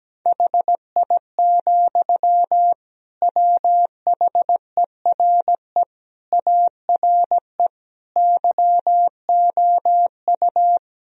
Morse code, heard as a keyed tone, about 17 wpm, 700 Hz. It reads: HI, WHERE ARE YOU
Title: Sentences from Top 1000 Words - Review Part 3 - 17wpm